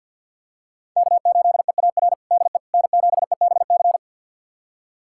Morse code, heard as a keyed tone, about 49 wpm, 700 Hz. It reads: K8EAR DE N7EBC
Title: USA Callsign DE Callsign - 5 Characters - 50wpm